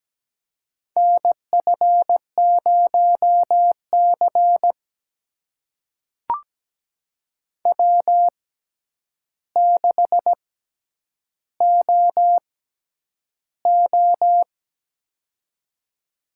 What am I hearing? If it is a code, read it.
Morse code, heard as a keyed tone, about 17 wpm, spaced as for 9 wpm, 700 Hz. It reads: NF0C W6OO
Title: USA Call Signs - 4 Characters - 6x Character Spacing - 17wpm